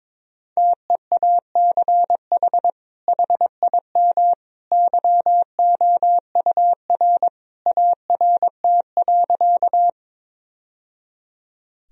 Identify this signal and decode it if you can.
Morse code, heard as a keyed tone, about 22 wpm, 700 Hz. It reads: TEACH HIM YOUR ART.
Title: Sentences from Top 1000 Words - Review Part 3 - 22wpm